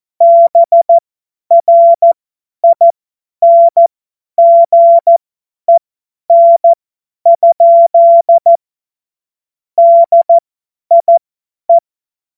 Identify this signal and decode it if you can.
Morse code, heard as a keyed tone, about 14 wpm, spaced as for 10 wpm, 680 Hz. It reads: BRINGEN? DIE
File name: Morsekurs / Witze / Witz 11. Juli 2023 50 F70